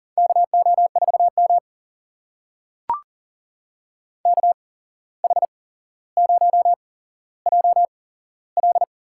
Morse code, heard as a keyed tone, about 42 wpm, spaced as for 16 wpm, 700 Hz. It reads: KO4M KH0JL
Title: USA Call Signs - 8x Character Spacing - 40wpm